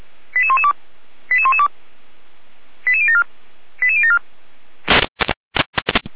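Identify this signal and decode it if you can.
Five-tone selective call (ZVEI), heard as quick runs of five tones, tones of about 70 ms. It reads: 88112 88112 88074 88074
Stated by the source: ZVEI five-tone selective call received with an SDR.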